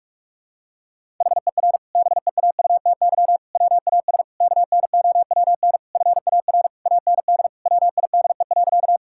Morse code, heard as a keyed tone, about 45 wpm, 700 Hz. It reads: HER BEAUTY WAS KNOWN FAR AND WIDE.
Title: Sentences from Top 1000 Words - Review Part 6 - 45wpm